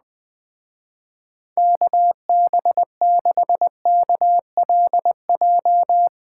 Morse code, heard as a keyed tone, about 20 wpm, 700 Hz. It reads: KB6KLJ
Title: Speed Racing - USA 2x3 Call Signs 17wpm